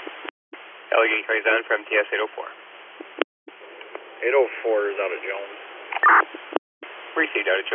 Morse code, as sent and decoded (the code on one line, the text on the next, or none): none